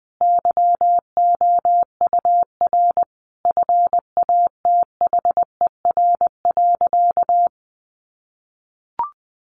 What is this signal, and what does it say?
Morse code, heard as a keyed tone, about 20 wpm, 700 Hz. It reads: YOUR FATHER.
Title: Sentences from Top 300 Words 20wpm